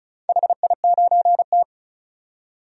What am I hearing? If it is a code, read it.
Morse code, heard as a keyed tone, about 34 wpm, 700 Hz. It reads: HI9T
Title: DXCC-100 Call Signs - 4 Characters - 35wpm